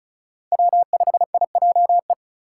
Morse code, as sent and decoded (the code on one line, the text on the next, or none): .-- ..... .. .--- .
W5IJE